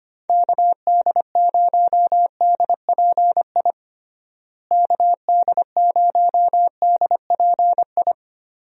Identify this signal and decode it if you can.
Morse code, heard as a keyed tone, about 25 wpm, 700 Hz. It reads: KD0DPI KD0DPI
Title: USA Call Signs - 6 Characters - Repeated 2x - 25wpm